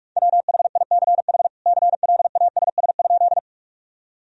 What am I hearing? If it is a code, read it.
Morse code, heard as a keyed tone, about 43 wpm, 700 Hz. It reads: WHICH CLASS?
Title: Sentences from Top 1000 Words - Review Part 2 - 45wpm